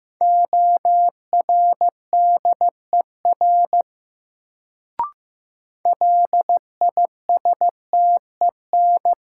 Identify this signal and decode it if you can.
Morse code, heard as a keyed tone, about 15 wpm, 700 Hz. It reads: ORDER LISTEN
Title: Sentences from Top 400 Words - New Word Drill - 15wpm